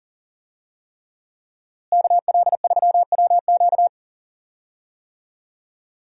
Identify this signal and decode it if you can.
Morse code, heard as a keyed tone, about 37 wpm, 700 Hz. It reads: KL3WQ
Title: USA Call Signs - 5 Characters - Repeated 3x - 40wpm